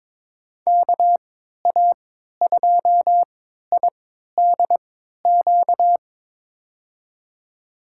Morse code, heard as a keyed tone, about 22 wpm, spaced as for 12 wpm, 700 Hz. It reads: KA2IDQ
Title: USA Call Signs - 6 Characters - 3x Character Spacing - 22wpm